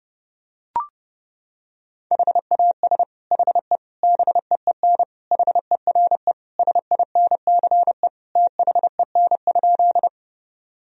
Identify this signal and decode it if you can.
Morse code, heard as a keyed tone, about 28 wpm, 700 Hz. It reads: HAS HE BEEN HERE SINCE THEN?
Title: Sentences from Top 1000 Words - Review Part 1 - 30wpm